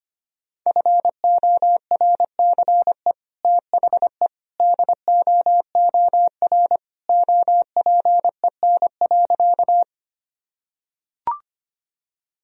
Morse code, heard as a keyed tone, about 25 wpm, 700 Hz. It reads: FORCE THE DOOR OPEN.
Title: Sentences from Top 1000 Words - Review Part 3 - 25wpm